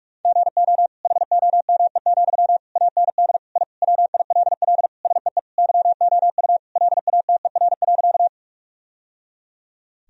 Morse code, heard as a keyed tone, about 41 wpm, 700 Hz. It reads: GO HOME, AND I WILL SEE YOU LATER.
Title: Sentences from Top 1000 Words - Review Part 4 - 45wpm